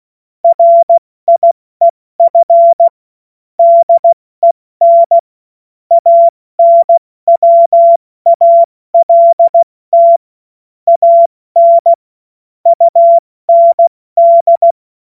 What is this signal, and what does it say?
Morse code, heard as a keyed tone, about 16 wpm, spaced as for 14 wpm, 680 Hz. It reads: RIEF DEN ANWALT AN UND
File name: Morsekurs / Witze / Witz 04. Oktober 2022 70 F80